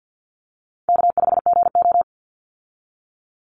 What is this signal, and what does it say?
Morse code, heard as a keyed tone, about 50 wpm, 700 Hz. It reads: K5GO
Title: Easy Call Signs - 50wpm